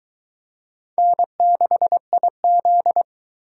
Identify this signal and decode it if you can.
Morse code, heard as a keyed tone, about 22 wpm, 700 Hz. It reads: N6IZ